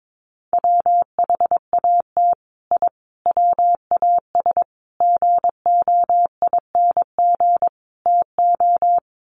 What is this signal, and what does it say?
Morse code, heard as a keyed tone, about 22 wpm, 700 Hz. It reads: WHAT I WAS GOING TO
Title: Sets of 5 Words Encore2 22wpm